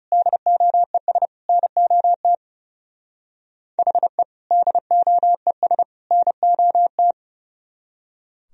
Morse code, heard as a keyed tone, about 36 wpm, 700 Hz. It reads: DOES NOT HE DOES NOT
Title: Speed-Racing - Sets of 3 Words 30wpm